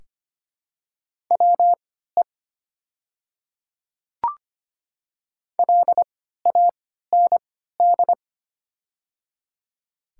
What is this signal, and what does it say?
Morse code, heard as a keyed tone, about 25 wpm, spaced as for 14 wpm, 700 Hz. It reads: WE LAND
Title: 3x Character Spacing - Top 200 Words - 25wpm